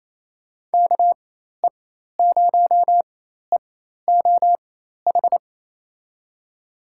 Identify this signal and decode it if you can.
Morse code, heard as a keyed tone, about 27 wpm, spaced as for 13 wpm, 700 Hz. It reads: KE0EOH
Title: USA Call Signs - 4x Character Spacing - 28wpm